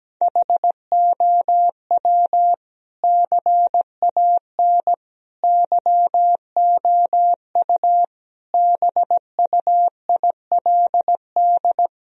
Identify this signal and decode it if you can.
Morse code, heard as a keyed tone, about 17 wpm, 700 Hz. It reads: HOW CAN YOU BUILD
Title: Sentences from Top 1000 Words - Review Part 1 - 17wpm